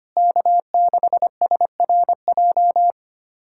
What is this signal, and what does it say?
Morse code, heard as a keyed tone, about 25 wpm, 700 Hz. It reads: K6SRJ
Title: USA Callsign DE Callsign - 5 Characters - 25wpm